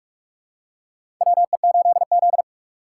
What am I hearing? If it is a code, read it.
Morse code, heard as a keyed tone, about 44 wpm, 690 Hz. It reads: WE8Z
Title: USA Call Signs - 4 Characters - 4x Character Spacing - 45wpm